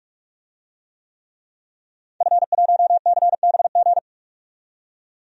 Morse code, heard as a keyed tone, about 42 wpm, 700 Hz. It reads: F1CBG
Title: DXCC-100 Call Signs - 5 Characters - 45wpm